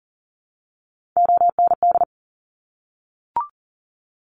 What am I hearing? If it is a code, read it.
Morse code, heard as a keyed tone, about 41 wpm, 700 Hz. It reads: OND
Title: Top Three Letter Combinations in Words 40wpm